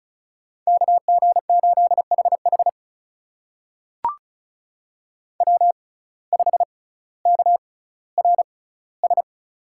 Morse code, heard as a keyed tone, about 36 wpm, spaced as for 18 wpm, 700 Hz. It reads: KG8HH W5KRS
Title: USA Call Signs - 5 Characters - 6x Character Spacing - 35wpm